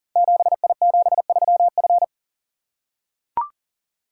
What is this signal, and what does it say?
Morse code, heard as a keyed tone, about 38 wpm, 700 Hz. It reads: 7I73F
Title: Code Groups - 5 Letter:Numbers - 40wpm